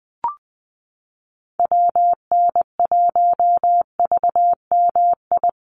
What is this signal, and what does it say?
Morse code, heard as a keyed tone, about 20 wpm, 700 Hz. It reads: WN1VMI
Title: USA Call Signs Repeated 3x 20wpm